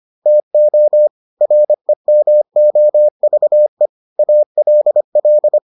Morse code, heard as a keyed tone, about 25 wpm, 600 Hz. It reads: TO REMOVE ALL